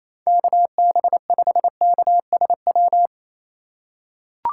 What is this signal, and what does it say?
Morse code, heard as a keyed tone, about 28 wpm, 700 Hz. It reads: KB5KSW
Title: USA Call Signs - 4x Character Spacing - 28wpm